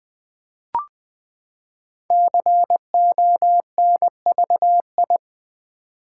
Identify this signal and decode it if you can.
Morse code, heard as a keyed tone, about 20 wpm, 700 Hz. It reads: CONVI
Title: Build A Word - 5 Letters At A Time - 10-Letter Words - 20wpm